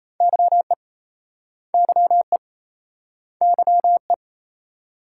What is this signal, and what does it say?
Morse code, heard as a keyed tone, about 36 wpm, 700 Hz. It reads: YE YE YE